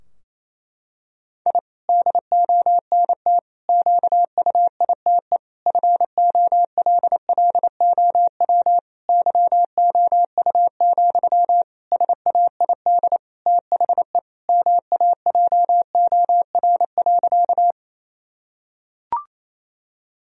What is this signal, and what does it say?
Morse code, heard as a keyed tone, about 28 wpm, 700 Hz. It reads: I DONT QUITE FOLLOW YOU, SAID THE MAJOR.
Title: Sentences from Top 1000 Words - Review Part 4 - 28wpm